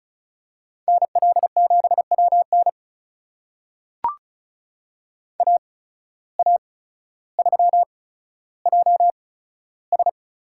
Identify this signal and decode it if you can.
Morse code, heard as a keyed tone, about 36 wpm, spaced as for 15 wpm, 700 Hz. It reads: NL7WN AA3JS
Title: USA Call Signs - 5 Characters - 8x Character Spacing - 35wpm